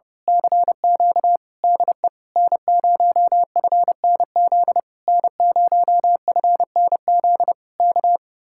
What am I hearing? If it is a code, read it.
Morse code, heard as a keyed tone, about 28 wpm, 700 Hz. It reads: CQ DE N0FNZ N0FNZ K
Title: Speed-Racing Common QSO Phrases Encore - 30wpm